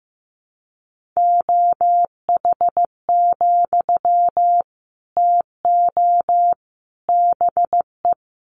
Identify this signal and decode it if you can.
Morse code, heard as a keyed tone, about 15 wpm, 700 Hz. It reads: OH, TO BE
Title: Sentences from Top 500 Words 15wpm